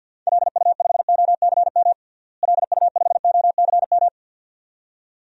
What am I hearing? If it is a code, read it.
Morse code, heard as a keyed tone, about 49 wpm, 700 Hz. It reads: LU5OCM LU5OCM
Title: DXCC-100 Call Signs - 6 Characters - Repeated 2x - 50wpm